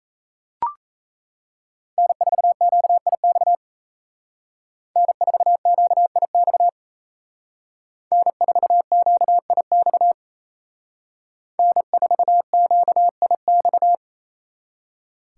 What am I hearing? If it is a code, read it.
Morse code, heard as a keyed tone, about 35 wpm, 700 Hz. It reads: N4QIX N4QIX N4QIX N4QIX
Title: Speed Racing - USA 1x3 Call Signs 28wpm